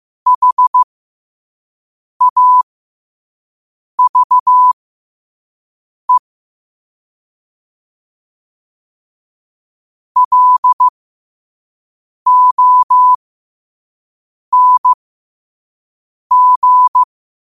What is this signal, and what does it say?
Morse code, heard as a keyed tone, about 15 wpm, spaced as for 5 wpm, 1000 Hz. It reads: HAVE LONG